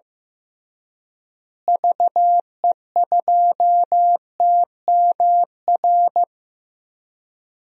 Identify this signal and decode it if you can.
Morse code, heard as a keyed tone, about 15 wpm, 700 Hz. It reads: VE2TMR
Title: DXCC-100 Call Signs - 6 Characters - Repeated 3x - 15wpm